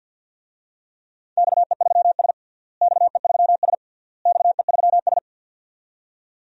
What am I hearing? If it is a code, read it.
Morse code, heard as a keyed tone, about 49 wpm, 700 Hz. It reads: XE3S XE3S XE3S
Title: DXCC-100 Call Signs - 4 Characters - Repeated 3x - 50wpm